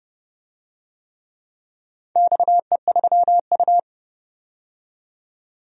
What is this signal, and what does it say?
Morse code, heard as a keyed tone, about 31 wpm, 700 Hz. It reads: XE3U
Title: DXCC-100 Call Signs - 4 Characters - 30wpm